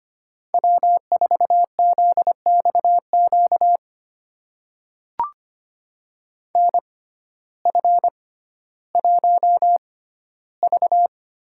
Morse code, heard as a keyed tone, about 25 wpm, spaced as for 14 wpm, 700 Hz. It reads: W4ZXQ NF1V
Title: USA Call Signs - 6x Character Spacing - 25wpm